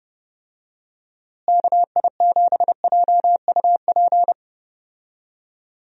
Morse code, heard as a keyed tone, about 30 wpm, 700 Hz. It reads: KI7JUP
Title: USA Call Signs - 6 Characters - Repeated 2x - 30wpm